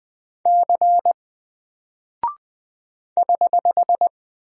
Morse code, heard as a keyed tone, about 20 wpm, 700 Hz. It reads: C <HH>
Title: Single Character (Letter:Number:Punctuation:Prosign) 20wpm